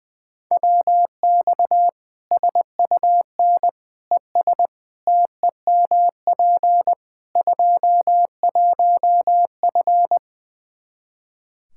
Morse code, heard as a keyed tone, about 20 wpm, 700 Hz. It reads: WX SUN ES TEMP 21F
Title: Practice QSO - Exchange 2 - Info - 20wpm